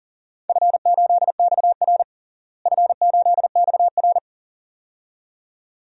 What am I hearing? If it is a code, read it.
Morse code, heard as a keyed tone, about 40 wpm, 700 Hz. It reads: F8XR F8XR
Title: DXCC-100 Call Signs - 4 Characters - Repeated 2x - 40wpm